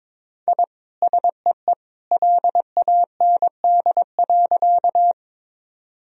Morse code, heard as a keyed tone, about 22 wpm, 700 Hz. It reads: I SEE LAND.
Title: Sentences from Top 1000 Words - Review Part 1 - 22wpm